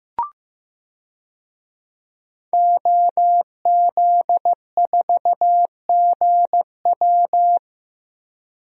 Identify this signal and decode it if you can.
Morse code, heard as a keyed tone, about 15 wpm, 700 Hz. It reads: OZ4GW
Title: DXCC-100 Call Signs - 5 Characters - Repeated 2x - 15wpm